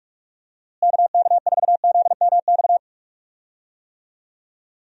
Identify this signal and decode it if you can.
Morse code, heard as a keyed tone, about 46 wpm, 700 Hz. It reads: KK4ZMX
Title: USA Call Signs - 6 Characters - Repeated 2x - 45wpm